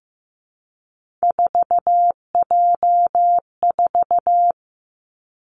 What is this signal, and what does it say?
Morse code, heard as a keyed tone, about 15 wpm, 700 Hz. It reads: 4J4
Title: Three Challenging Letter-Number Combinations 15wpm